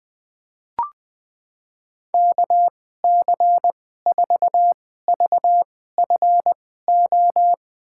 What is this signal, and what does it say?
Morse code, heard as a keyed tone, about 20 wpm, spaced as for 14 wpm, 700 Hz. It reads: KC4VFO
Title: USA Call Signs - 2x Character Spacing - 20wpm